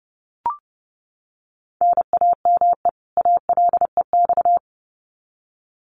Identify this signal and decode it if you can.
Morse code, heard as a keyed tone, about 30 wpm, 700 Hz. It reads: NAME ALEX
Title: Common QSO Phrases 30wpm